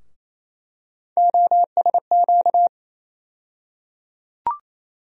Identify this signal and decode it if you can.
Morse code, heard as a keyed tone, about 28 wpm, 700 Hz. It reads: OSQ
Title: USA Call Sign Postfixes - 3 Letters 28wpm